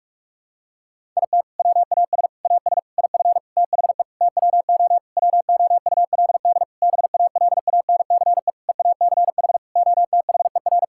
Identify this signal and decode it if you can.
Morse code, heard as a keyed tone, about 44 wpm, 700 Hz. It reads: IT WAS AS IF THE TWO WOULD BALANCE EACH OTHER
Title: Sentences from Top 1000 Words - Review Part 6 - 45wpm